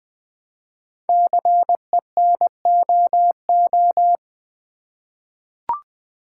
Morse code, heard as a keyed tone, about 20 wpm, 700 Hz. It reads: CENOO
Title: Code Groups - 5 Letters - 20wpm